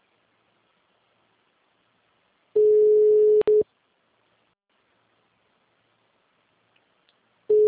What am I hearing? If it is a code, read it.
Ringback tone (Europe, 425 Hz).
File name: out- -2038- - - .30792